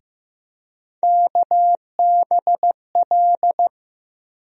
Morse code, heard as a keyed tone, about 15 wpm, 700 Hz. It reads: KBL
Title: USA Call Sign Postfixes - 3 Letters 15wpm